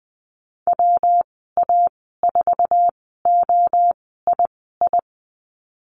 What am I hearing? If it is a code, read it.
Morse code, heard as a keyed tone, about 20 wpm, spaced as for 15 wpm, 700 Hz. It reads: WA4OII
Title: 122 Lesson 20g - Number 3 - Call Signs 20wpm